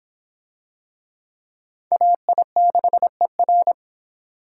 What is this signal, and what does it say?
Morse code, heard as a keyed tone, about 25 wpm, 700 Hz. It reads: AI6ER